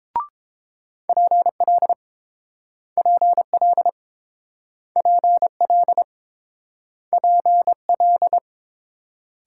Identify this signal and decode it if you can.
Morse code, heard as a keyed tone, about 30 wpm, 690 Hz. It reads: PL PL PL PL